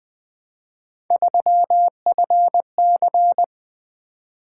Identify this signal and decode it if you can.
Morse code, heard as a keyed tone, about 20 wpm, 700 Hz. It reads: 3FC
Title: Three Letter-Number Combinations - 20wpm